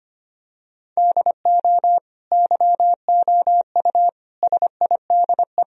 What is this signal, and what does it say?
Morse code, heard as a keyed tone, about 25 wpm, 700 Hz. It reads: DO YOU SIDE
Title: Sentences from Top 1000 Words - Review Part 1 - 25wpm